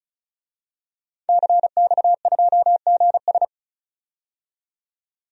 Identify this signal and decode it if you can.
Morse code, heard as a keyed tone, about 36 wpm, 700 Hz. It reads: CX2GS
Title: DXCC-100 Call Signs - 5 Characters - 35wpm